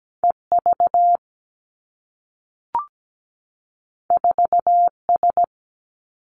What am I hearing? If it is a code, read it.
Morse code, heard as a keyed tone, about 17 wpm, 700 Hz. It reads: E V 4 S
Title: Two Challenging Letter-Number Combinations 17wpm